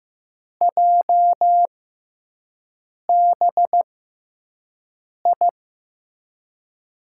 Morse code, heard as a keyed tone, about 15 wpm, spaced as for 5 wpm, 700 Hz. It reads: JBI
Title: USA Call Signs - 6 Characters - 6x Character Spacing - 15wpm